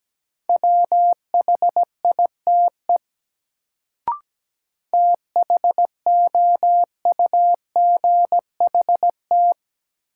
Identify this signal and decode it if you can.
Morse code, heard as a keyed tone, about 17 wpm, 700 Hz. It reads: WHITE THOUGHT
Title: Sets of 2 Words - New Word Drill - 17wpm